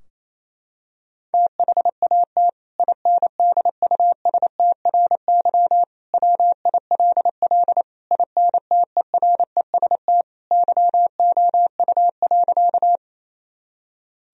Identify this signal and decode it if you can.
Morse code, heard as a keyed tone, about 28 wpm, 700 Hz. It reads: THAT INDUSTRY WILL INTEREST YOU.
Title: Sentences from Top 1000 Words - Review Part 4 - 28wpm